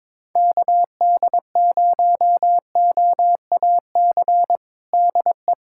Morse code, heard as a keyed tone, about 22 wpm, 700 Hz. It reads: KD0OAC DE
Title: USA Callsign DE Callsign 22wpm